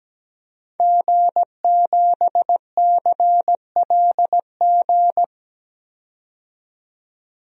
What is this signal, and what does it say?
Morse code, heard as a keyed tone, about 17 wpm, 700 Hz. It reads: G7CLG
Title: DXCC-100 Call Signs - 5 Characters - 17wpm